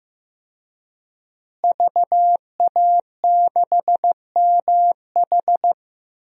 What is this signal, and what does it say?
Morse code, heard as a keyed tone, about 15 wpm, 700 Hz. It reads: VA6MH
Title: DXCC-100 Call Signs - 5 Characters - Repeated 3x - 15wpm